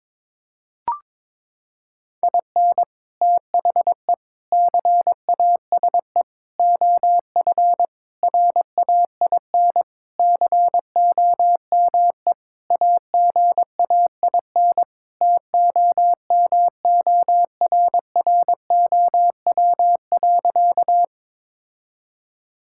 Morse code, heard as a keyed tone, about 22 wpm, 700 Hz. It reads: IN THE CASE OF RAIN COME AGAIN TOMORROW.
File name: practice / 22wpm / Sentences with 4-Word Phrases - 22wpm